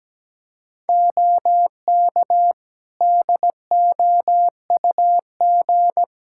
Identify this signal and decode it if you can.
Morse code, heard as a keyed tone, about 17 wpm, 700 Hz. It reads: OK DOUG
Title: K1USN SST QSO Practice 17wpm